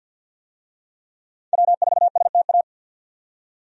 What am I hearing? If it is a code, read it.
Morse code, heard as a keyed tone, about 52 wpm, 700 Hz. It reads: W4STA
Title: USA Call Signs - 5 Characters - 50wpm